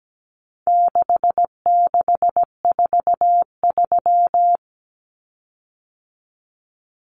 Morse code, heard as a keyed tone, about 17 wpm, 700 Hz. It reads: 6643
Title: Common Cut Numbers - 4-Digit 17wpm